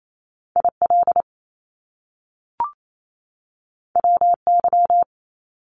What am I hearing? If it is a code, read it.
Morse code, heard as a keyed tone, about 28 wpm, 700 Hz. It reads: IL WY
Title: US States 28wpm